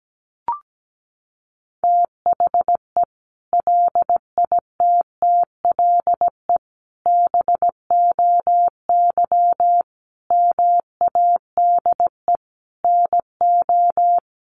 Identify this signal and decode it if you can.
Morse code, heard as a keyed tone, about 17 wpm, 700 Hz. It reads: THE LITTLE BOY MADE NO
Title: Sentences from Top 1400 Words 17wpm